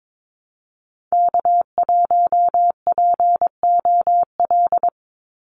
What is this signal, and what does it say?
Morse code, heard as a keyed tone, about 22 wpm, 700 Hz. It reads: K1POL
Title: USA Call Signs 22wpm